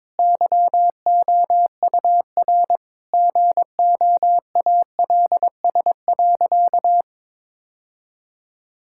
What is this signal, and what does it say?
Morse code, heard as a keyed tone, about 22 wpm, 700 Hz. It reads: YOUR GOALS.
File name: Difficult Sentences with Hints - 22wpm